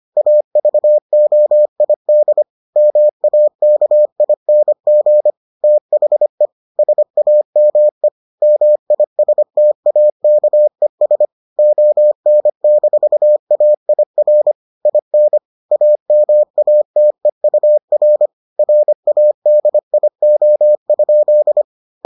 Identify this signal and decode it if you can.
Morse code, heard as a keyed tone, about 25 wpm, 600 Hz. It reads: AVOID MAKING THE SAME MISTAKES ON-AIR IN AMATEUR RADIO?